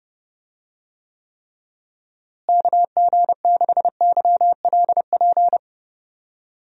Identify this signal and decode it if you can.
Morse code, heard as a keyed tone, about 30 wpm, 700 Hz. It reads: KG6YLP
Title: USA Call Signs - 6 Characters - Repeated 3x - 30wpm